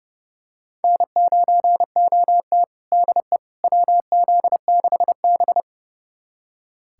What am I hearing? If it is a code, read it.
Morse code, heard as a keyed tone, about 30 wpm, 700 Hz. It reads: N9OT DE WZ6B
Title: USA Callsign DE Callsign - 4 Characters - 30wpm